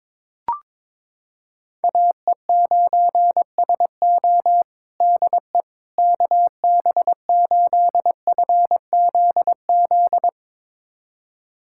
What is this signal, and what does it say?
Morse code, heard as a keyed tone, about 22 wpm, 700 Hz. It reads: AE9SO DE KB8FZZ
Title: USA Callsign DE Callsign 22wpm